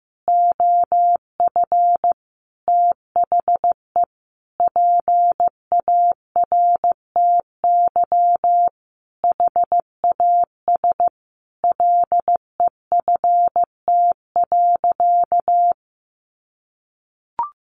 Morse code, heard as a keyed tone, about 15 wpm, 700 Hz. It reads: OF THE PARTY HAS LEFT.
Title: Sentences from Top 1500 Words 15wpm